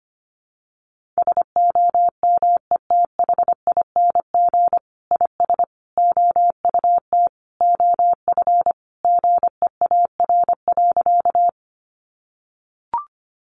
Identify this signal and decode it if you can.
Morse code, heard as a keyed tone, about 25 wpm, 700 Hz. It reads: SOMETHING IS OUT OF GEAR.
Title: Sentences from Top 1400 Words 25wpm